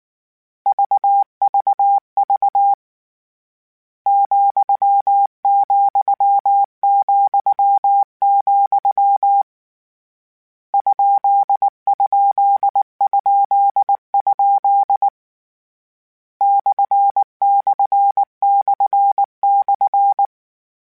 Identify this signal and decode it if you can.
Morse code, heard as a keyed tone, about 19 wpm, 800 Hz. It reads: VVV ,,,, ???? ////
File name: Misc / Misc-mp3s / Punct @18